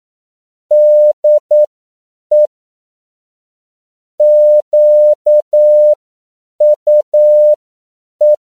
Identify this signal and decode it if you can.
Morse code, heard as a keyed tone, about 9 wpm, spaced as for 7 wpm, 600 Hz. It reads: DE QUE
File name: CW / 10PPM